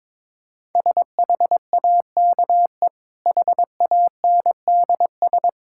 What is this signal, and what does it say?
Morse code, heard as a keyed tone, about 22 wpm, 690 Hz. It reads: SHAKE HANDS
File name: practice / 22wpm / Easy Sentences with Hints - 22wpm